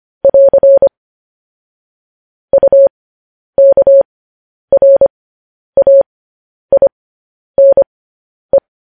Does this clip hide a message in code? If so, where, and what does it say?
Yes: Morse code from 0.2 to 8.6 s, heard as a keyed tone, about 25 wpm, spaced as for 10 wpm, 550 Hz. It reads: <AR> UKRAINE